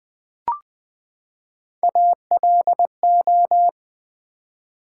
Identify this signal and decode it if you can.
Morse code, heard as a keyed tone, about 20 wpm, 700 Hz. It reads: ALO
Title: Build A Word - 3 Letters At A Time - 9-Letter Words - 20wpm